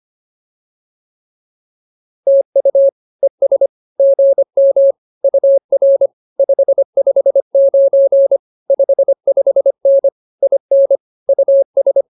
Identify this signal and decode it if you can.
Morse code, heard as a keyed tone, about 24 wpm, 550 Hz. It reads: TU ES GM UR 559 55N IN US
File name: POTA306a 25